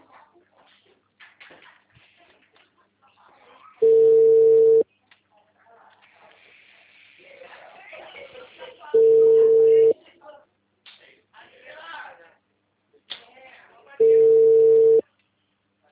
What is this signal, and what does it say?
Ringback tone (China, 450 Hz), with speech.
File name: out- -2016